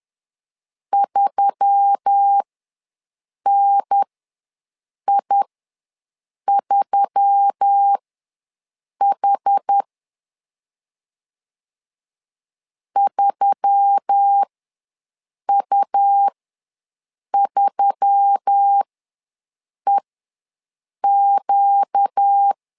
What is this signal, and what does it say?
Morse code, heard as a keyed tone, about 10 wpm, spaced as for 6 wpm, 800 Hz. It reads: 3NI3H 3U3EQ
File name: LEZIONI / 4 Q1S3 ws 10wpm 2437